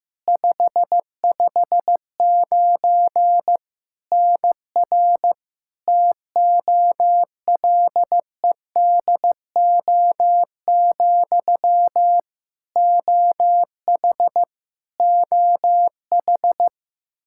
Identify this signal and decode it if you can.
Morse code, heard as a keyed tone, about 15 wpm, 700 Hz. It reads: 559 NR TOLEDO, OH OH
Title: Practice QSO - Exchange 1 - Beginning - 15wpm